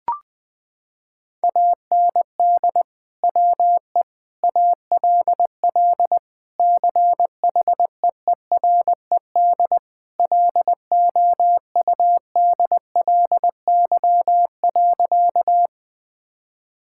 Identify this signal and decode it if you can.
Morse code, heard as a keyed tone, about 20 wpm, 700 Hz. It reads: AND WE ALL CHEERED LOUDLY.
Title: Sentences with 2-Word Phrases - 20wpm